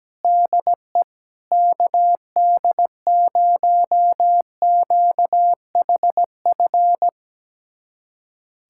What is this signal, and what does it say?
Morse code, heard as a keyed tone, about 17 wpm, 700 Hz. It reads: DE KD0QHF
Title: USA Callsign DE Callsign - 6 Characters - 17wpm